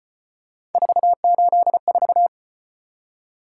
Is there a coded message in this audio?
Yes: Morse code from 0.7 to 2.3 s, heard as a keyed tone, about 34 wpm, 700 Hz. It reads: 484